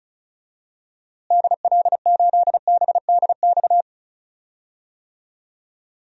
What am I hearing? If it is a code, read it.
Morse code, heard as a keyed tone, about 36 wpm, 700 Hz. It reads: DL8BDX
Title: DXCC-100 Call Signs - 6 Characters - Repeated 3x - 35wpm